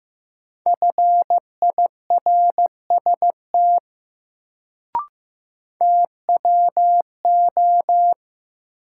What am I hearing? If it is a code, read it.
Morse code, heard as a keyed tone, about 15 wpm, 700 Hz. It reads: FIRST TWO
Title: Sentences from Top 100 Words - New Word Drill - 15wpm